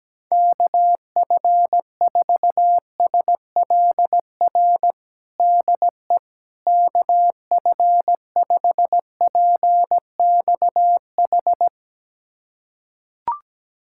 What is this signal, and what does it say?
Morse code, heard as a keyed tone, about 17 wpm, 700 Hz. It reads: KF4SLR DE KF5PXH
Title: USA Callsign DE Callsign - 6 Characters - 17wpm